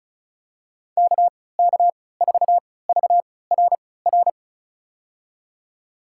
Morse code, heard as a keyed tone, about 36 wpm, spaced as for 20 wpm, 700 Hz. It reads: KK4VRR
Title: USA Call Signs - 6 Characters - 3x Character Spacing - 35wpm